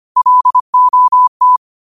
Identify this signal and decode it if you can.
Morse code, heard as a keyed tone, about 25 wpm, 1000 Hz. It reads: LOT
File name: qotdfiles / 25wpm